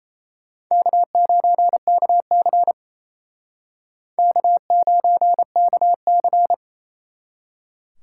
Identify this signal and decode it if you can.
Morse code, heard as a keyed tone, about 33 wpm, 700 Hz. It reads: K9KC K9KC
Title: Speed Racing - USA 1x2 Call Signs 28wpm